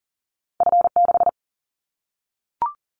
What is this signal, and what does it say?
Morse code, heard as a keyed tone, about 40 wpm, 700 Hz. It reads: F6
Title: Callsign Prefixes for Top 50 DXCC - 40wpm